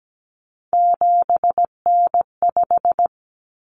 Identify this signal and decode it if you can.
Morse code, heard as a keyed tone, about 17 wpm, 700 Hz. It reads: 7N5
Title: Common Cut Numbers - 3-Digit 17wpm